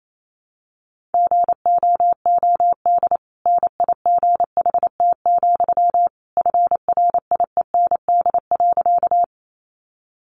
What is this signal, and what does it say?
Morse code, heard as a keyed tone, about 28 wpm, 700 Hz. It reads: GOOD NIGHT, FRIEND.
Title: Sentences from Top 300 Words 28wpm